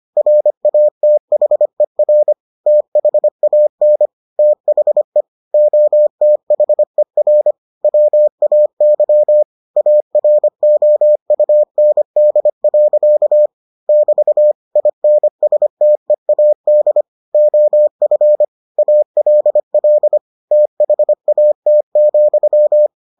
Morse code, heard as a keyed tone, about 25 wpm, 600 Hz. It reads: RATHER THAN THE OTHER WAY AROUND. <BT> INSTEAD OF ALL THAT,